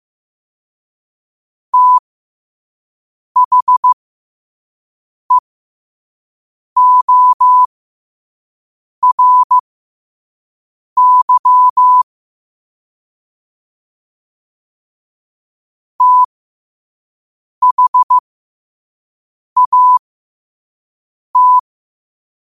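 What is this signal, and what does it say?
Morse code, heard as a keyed tone, about 15 wpm, spaced as for 5 wpm, 1000 Hz. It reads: THEORY THAT
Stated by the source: 5 WPM Morse quote of the day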